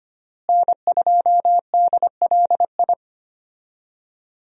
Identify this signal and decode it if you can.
Morse code, heard as a keyed tone, about 25 wpm, 700 Hz. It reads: N2DLI
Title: USA Call Signs - 5 Characters - Repeated 2x - 25wpm